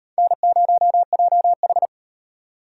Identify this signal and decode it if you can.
Morse code, heard as a keyed tone, about 38 wpm, 700 Hz. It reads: N0JH